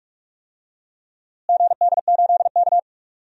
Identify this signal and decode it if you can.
Morse code, heard as a keyed tone, about 46 wpm, 700 Hz. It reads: GD8K
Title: DXCC-100 Call Signs - 4 Characters - Repeated 3x - 45wpm